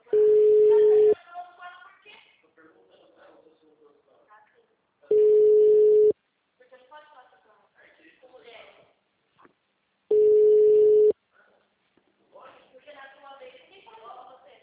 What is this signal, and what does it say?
Ringback tone (Europe, 425 Hz), with speech.